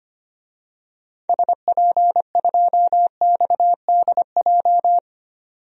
Morse code, heard as a keyed tone, about 25 wpm, 690 Hz. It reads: SP2XDJ
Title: DXCC-100 Call Signs - 6 Characters - 25wpm